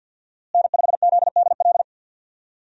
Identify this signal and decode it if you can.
Morse code, heard as a keyed tone, about 47 wpm, 700 Hz. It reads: N5ZDL
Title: USA Call Signs - 5 Characters - Repeated 3x - 50wpm